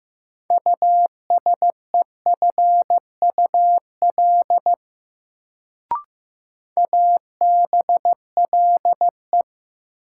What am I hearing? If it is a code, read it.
Morse code, heard as a keyed tone, about 15 wpm, 700 Hz. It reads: USEFUL ABLE
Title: Sentences from Top 700 Words - New Word Drill - 15wpm